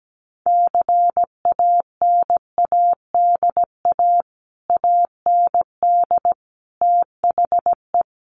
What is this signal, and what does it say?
Morse code, heard as a keyed tone, about 17 wpm, 700 Hz. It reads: CANADA AND THE
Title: Sets of 5 Words Encore 17wpm